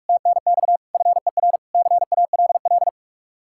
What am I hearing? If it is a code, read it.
Morse code, heard as a keyed tone, about 60 wpm, spaced as for 53 wpm, 700 Hz. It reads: TNX FER CALL
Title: Speed-Racing Common QSO Phrases - 45wpm